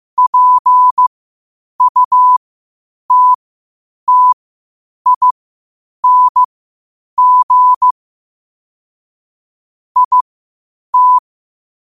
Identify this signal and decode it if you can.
Morse code, heard as a keyed tone, about 15 wpm, spaced as for 8 wpm, 1000 Hz. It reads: PUTTING IT
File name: qotdfiles / 8wpm